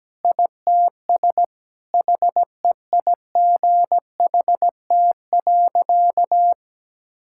Morse code, heard as a keyed tone, about 17 wpm, 700 Hz. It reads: ITS HEIGHT.
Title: Sentences from Top 1600 Words 17wpm